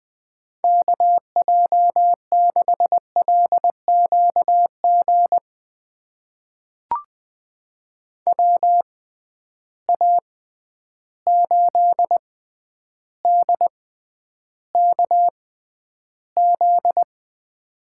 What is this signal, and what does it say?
Morse code, heard as a keyed tone, about 21 wpm, spaced as for 10 wpm, 700 Hz. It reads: KJ6LQG WA8DKZ
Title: USA Call Signs - 6 Characters - 6x Character Spacing - 20wpm